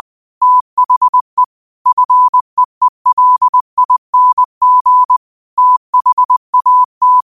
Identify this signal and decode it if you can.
Morse code, heard as a keyed tone, about 20 wpm, 1000 Hz. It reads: THE FEELING THAT
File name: qotdfiles / 20wpm